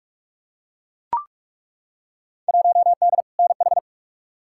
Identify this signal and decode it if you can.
Morse code, heard as a keyed tone, about 42 wpm, 700 Hz. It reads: 1D NH
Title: Field Day Exchanges - 45wpm